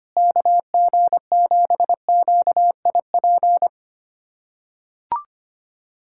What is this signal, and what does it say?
Morse code, heard as a keyed tone, about 25 wpm, 700 Hz. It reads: KG7QIP
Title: USA Callsign DE Callsign 25wpm